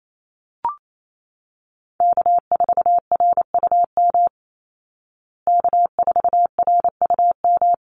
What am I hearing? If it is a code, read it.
Morse code, heard as a keyed tone, about 28 wpm, 700 Hz. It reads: K4RUM K4RUM
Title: Call Signs Repeated 3x - 28wpm